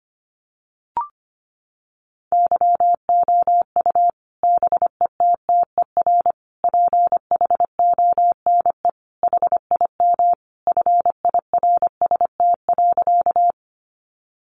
Morse code, heard as a keyed tone, about 25 wpm, 700 Hz. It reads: YOU BETTER PHONE HIM FIRST.
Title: Sentences from Top 1100 Words 25wpm